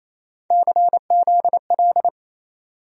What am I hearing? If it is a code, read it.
Morse code, heard as a keyed tone, about 28 wpm, 700 Hz. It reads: CZL
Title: USA Call Sign Postfixes - 3 Letters 28wpm